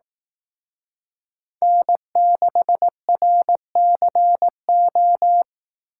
Morse code, heard as a keyed tone, about 18 wpm, 700 Hz. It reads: N6RCO
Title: Speed Racing - USA Call Signs 15wpm